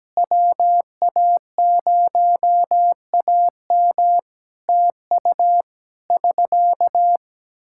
Morse code, heard as a keyed tone, about 17 wpm, 700 Hz. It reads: WA0AM TU <SK>
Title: Practice QSO - Exchange 4 - Ending - 17wpm